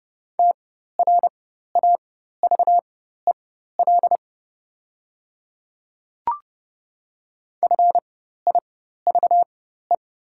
Morse code, heard as a keyed tone, about 30 wpm, spaced as for 14 wpm, 700 Hz. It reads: TRAVEL FIVE
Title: 4x Character Spacing - Top 500 Words - 30wpm